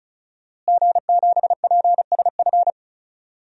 Morse code, heard as a keyed tone, about 34 wpm, 700 Hz. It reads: G7PSF
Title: DXCC-100 Call Signs - 5 Characters - Repeated 3x - 35wpm